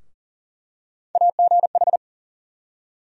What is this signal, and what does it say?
Morse code, heard as a keyed tone, about 40 wpm, 700 Hz. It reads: AGH
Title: USA Call Sign Postfixes - 3 Letters 40wpm